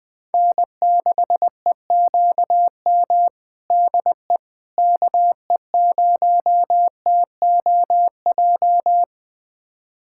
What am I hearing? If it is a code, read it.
Morse code, heard as a keyed tone, about 20 wpm, 700 Hz. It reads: N6EQM DE KE0TOJ
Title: USA Callsign DE Callsign 20wpm